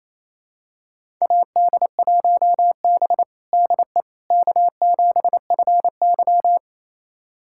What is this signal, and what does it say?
Morse code, heard as a keyed tone, about 28 wpm, 700 Hz. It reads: AD1B DE K7FY
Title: USA Callsign DE Callsign - 4 Characters - 28wpm